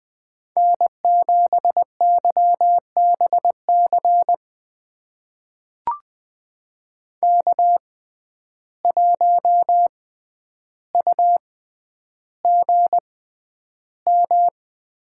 Morse code, heard as a keyed tone, about 20 wpm, spaced as for 10 wpm, 700 Hz. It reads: N7YBC K1UGM
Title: USA Call Signs - 6x Character Spacing - 20wpm